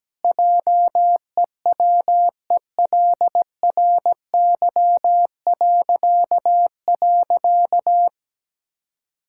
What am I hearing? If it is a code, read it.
Morse code, heard as a keyed tone, about 17 wpm, 700 Hz. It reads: JEWELRY..
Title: Pangram Challenge v2 - 17wpm